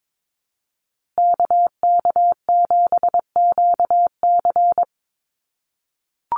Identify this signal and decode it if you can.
Morse code, heard as a keyed tone, about 22 wpm, 700 Hz. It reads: KK7QC
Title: USA Call Signs Repeated 2x 22wpm